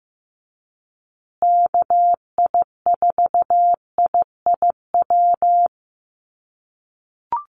Morse code, heard as a keyed tone, about 15 wpm, 700 Hz. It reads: KI4IIW
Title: USA Call Signs Repeated 3x 15wpm